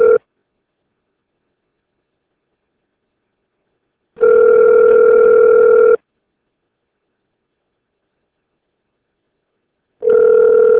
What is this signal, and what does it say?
Ringback tone (North America), 440 + 480 Hz.